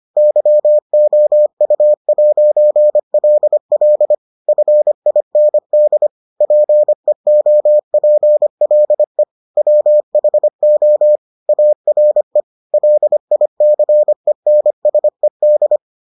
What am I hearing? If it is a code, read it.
Morse code, heard as a keyed tone, about 25 wpm, 600 Hz. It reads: YOU'LL FIND PEOPLE WHO ARE LICENSED